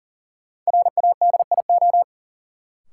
Morse code, heard as a keyed tone, about 40 wpm, 700 Hz. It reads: RADIO
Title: Sentences from Top 1000 Words - Review Part 3 - New Word Drill - 40wpm